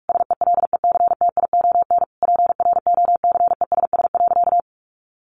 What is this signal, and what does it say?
Morse code, heard as a keyed tone, about 44 wpm, 700 Hz. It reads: SELECTION PROCESS.
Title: Sentences from Top 1400 Words 45wpm